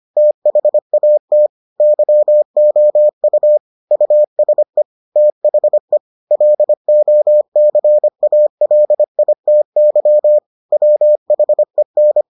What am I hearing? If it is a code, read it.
Morse code, heard as a keyed tone, about 25 wpm, 600 Hz. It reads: THAT YOU USE THE LOCALITY WHEN